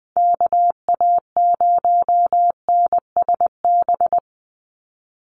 Morse code, heard as a keyed tone, about 20 wpm, 700 Hz. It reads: KA0NSB
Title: USA Call Signs 20wpm